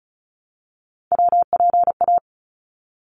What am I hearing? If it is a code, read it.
Morse code, heard as a keyed tone, about 34 wpm, 700 Hz. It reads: WPA
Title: ARRL Sections 35wpm